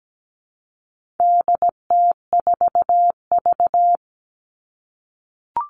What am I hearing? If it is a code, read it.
Morse code, heard as a keyed tone, about 17 wpm, 700 Hz. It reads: DT4V
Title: All Cut Numbers - 4-Digit 17wpm